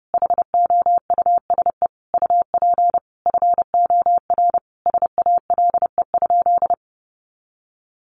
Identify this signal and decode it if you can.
Morse code, heard as a keyed tone, about 30 wpm, 700 Hz. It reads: HOUSE UP FOR SALE?
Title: Sentences from Top 1400 Words 30wpm